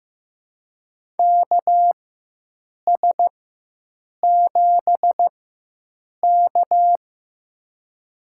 Morse code, heard as a keyed tone, about 15 wpm, spaced as for 7 wpm, 700 Hz. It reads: KS7K
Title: USA Call Signs - 4x Character Spacing - 15wpm